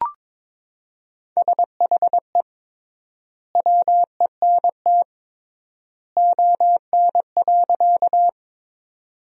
Morse code, heard as a keyed tone, about 22 wpm, 700 Hz. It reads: SHE WENT ON.
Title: 3x Word-Spacing - Sentences from Top 200 Words - 22wpm